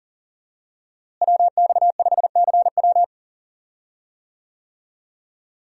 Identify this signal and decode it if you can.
Morse code, heard as a keyed tone, about 40 wpm, 700 Hz. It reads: WX5CW
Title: USA Call Signs - 5 Characters - Repeated 2x - 40wpm